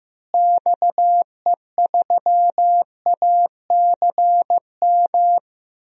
Morse code, heard as a keyed tone, about 16 wpm, 700 Hz. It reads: XE3ACM
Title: DXCC-100 Call Signs - 6 Characters - Repeated 3x - 15wpm